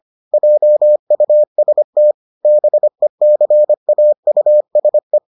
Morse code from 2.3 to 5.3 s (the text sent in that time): BECAUSE